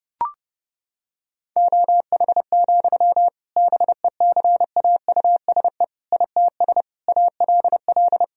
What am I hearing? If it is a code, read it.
Morse code, heard as a keyed tone, about 32 wpm, 700 Hz. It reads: OH, BECAUSE ITS ALL
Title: Sentences from Top 1000 Words - Review Part 3 - 30wpm